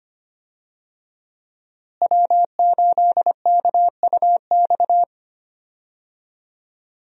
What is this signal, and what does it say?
Morse code, heard as a keyed tone, about 25 wpm, 700 Hz. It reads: W8KUX
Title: USA Call Signs - 5 Characters - Repeated 2x - 25wpm